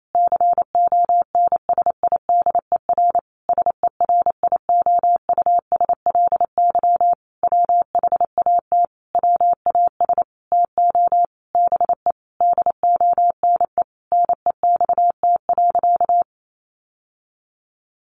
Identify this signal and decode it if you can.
Morse code, heard as a keyed tone, about 28 wpm, 700 Hz. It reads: CONSIDER SERIOUSLY WHAT WAS TO BE DONE NEXT.
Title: Sentences from Top 1200 Words 28wpm